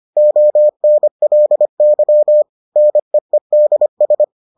Morse code, heard as a keyed tone, about 25 wpm, 600 Hz. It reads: ONLY NEEDS